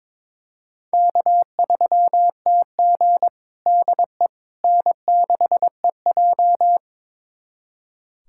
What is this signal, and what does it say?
Morse code, heard as a keyed tone, about 22 wpm, 700 Hz. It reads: K3TG DE N6EJ
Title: USA Callsign DE Callsign - 4 Characters - 22wpm